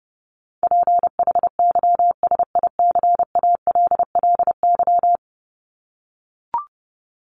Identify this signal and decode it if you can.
Morse code, heard as a keyed tone, about 30 wpm, 700 Hz. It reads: PHYSICALLY
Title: Top 1501-1750 Words 30wpm